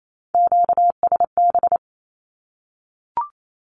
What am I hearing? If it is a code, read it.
Morse code, heard as a keyed tone, about 28 wpm, 700 Hz. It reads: QSB
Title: Q-Codes 28wpm